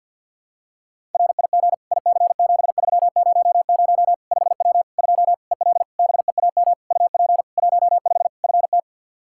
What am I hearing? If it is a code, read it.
Morse code, heard as a keyed tone, about 52 wpm, 700 Hz. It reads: RIG IC7300 5W 2 EL BEAM UP 15 FT
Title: K1USN SST QSO Practice 50wpm